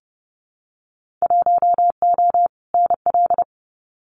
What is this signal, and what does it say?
Morse code, heard as a keyed tone, about 30 wpm, 700 Hz. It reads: 1O NL
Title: Winter Field Day Exchanges Encore 30wpm